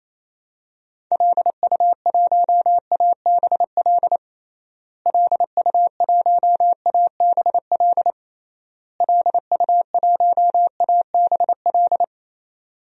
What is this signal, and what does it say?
Morse code, heard as a keyed tone, about 28 wpm, 700 Hz. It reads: LU1ABL LU1ABL LU1ABL
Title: DXCC-100 Call Signs - 6 Characters - Repeated 3x - 28wpm